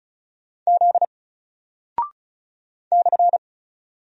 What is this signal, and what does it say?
Morse code, heard as a keyed tone, about 36 wpm, 700 Hz. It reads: Z /
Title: Single Character (Letter:Number:Punctuation:Prosign) Alt 35wpm